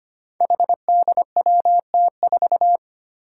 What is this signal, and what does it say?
Morse code, heard as a keyed tone, about 25 wpm, 700 Hz. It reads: HDWT4
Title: Five Challenging Letter-Number Combinations 25wpm